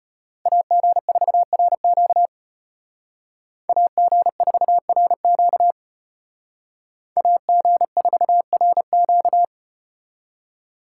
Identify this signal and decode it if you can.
Morse code, heard as a keyed tone, about 34 wpm, 700 Hz. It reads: AG4RQ AG4RQ AG4RQ